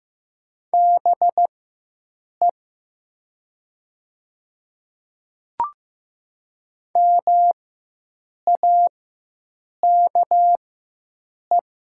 Morse code, heard as a keyed tone, about 15 wpm, spaced as for 7 wpm, 700 Hz. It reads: BE MAKE
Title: 4x Character Spacing - Top 100 Words - 15wpm